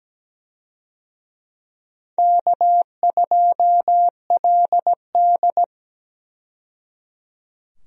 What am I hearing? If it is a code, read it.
Morse code, heard as a keyed tone, about 17 wpm, 700 Hz. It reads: K2LD
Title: USA Call Signs - 4 Characters - 17wpm